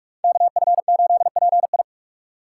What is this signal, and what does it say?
Morse code, heard as a keyed tone, about 45 wpm, 690 Hz. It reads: KF8PI
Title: USA Call Signs - 5 Characters - 2x Character Spacing - 45wpm